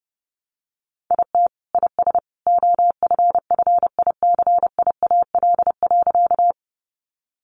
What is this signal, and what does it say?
Morse code, heard as a keyed tone, about 30 wpm, 700 Hz. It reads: IT IS OFFICIAL.
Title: Sentences from Top 1000 Words 30wpm